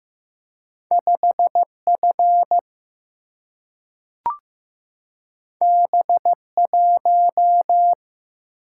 Morse code, heard as a keyed tone, about 15 wpm, 700 Hz. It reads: 5F B1
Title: Single-Letter Single-Number 15wpm